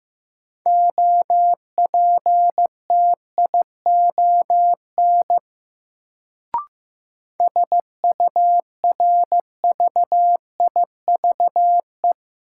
Morse code, heard as a keyed tone, about 15 wpm, 700 Hz. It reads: OPTION SURVIVE
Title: Sentences from Top 1300 Words - New Word Drill - 15wpm